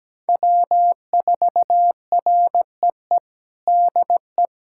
Morse code, heard as a keyed tone, about 17 wpm, 700 Hz. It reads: W4REE DE
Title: USA Callsign DE Callsign - 5 Characters - 17wpm